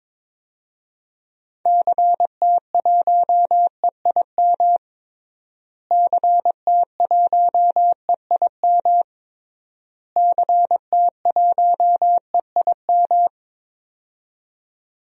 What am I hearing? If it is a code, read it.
Morse code, heard as a keyed tone, about 22 wpm, 700 Hz. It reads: CT1EIM CT1EIM CT1EIM
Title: DXCC-100 Call Signs - 6 Characters - Repeated 3x - 22wpm